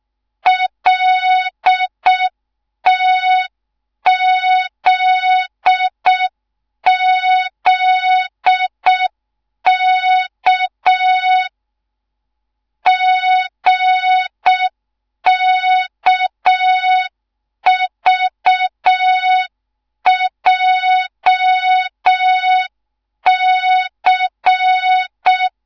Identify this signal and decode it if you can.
Morse code, heard as a keyed tone, about 6 wpm, 3700 Hz. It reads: LTZZK GKVJC
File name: Morze-HA2MN / hang / 13IRASJELEK1 HA2MN